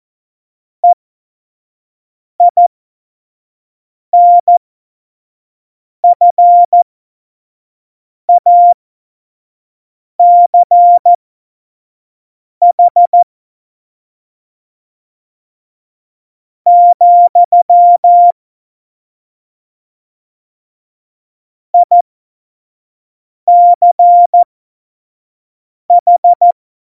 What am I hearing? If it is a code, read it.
Morse code, heard as a keyed tone, about 14 wpm, spaced as for 4 wpm, 700 Hz. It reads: EINFACH, ICH
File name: Morsekurs / Witze / Witz 15. Dezember, F70